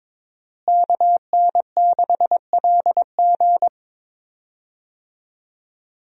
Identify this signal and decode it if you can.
Morse code, heard as a keyed tone, about 22 wpm, 700 Hz. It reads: KN6LG
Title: USA Call Signs - 5 Characters - Repeated 2x - 22wpm